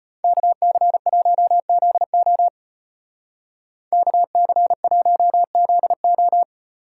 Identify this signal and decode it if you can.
Morse code, heard as a keyed tone, about 31 wpm, 700 Hz. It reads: KC1ZO KC1ZO
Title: Speed Racing - USA 2x2 Call Signs 25wpm